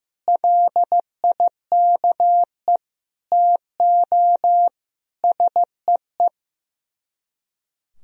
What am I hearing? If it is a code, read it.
Morse code, heard as a keyed tone, about 15 wpm, 700 Hz. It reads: LIKE TO SEE
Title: Speed-Racing - Sets of 4 Words 15wpm